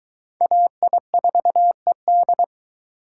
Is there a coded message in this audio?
Yes: Morse code from 0.4 to 2.4 s, heard as a keyed tone, about 21 wpm, 700 Hz. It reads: AI4ED